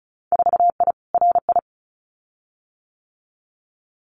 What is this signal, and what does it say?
Morse code, heard as a keyed tone, about 36 wpm, 700 Hz. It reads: 4I RI
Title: Winter Field Day Exchanges Encore 35wpm